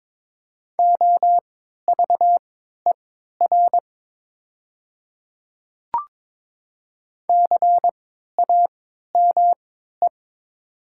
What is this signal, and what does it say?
Morse code, heard as a keyed tone, about 22 wpm, spaced as for 13 wpm, 700 Hz. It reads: OVER CAME
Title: 3x Character Spacing - Top 300 Words - 22wpm